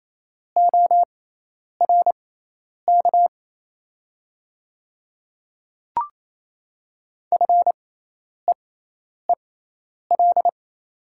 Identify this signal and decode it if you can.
Morse code, heard as a keyed tone, about 28 wpm, spaced as for 10 wpm, 700 Hz. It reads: ORK FEEL
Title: 6x Character Spacing - Top 500 Words - 28wpm